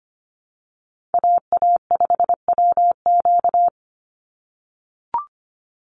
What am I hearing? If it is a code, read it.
Morse code, heard as a keyed tone, about 25 wpm, 700 Hz. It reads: AA5WQ
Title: USA Call Signs 25wpm